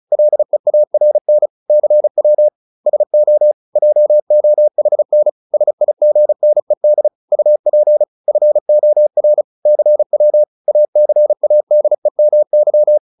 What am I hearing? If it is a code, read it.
Morse code, heard as a keyed tone, about 34 wpm, 600 Hz. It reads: LEARN CW SO JOHN SIGNED UP FOR CW ACADEMY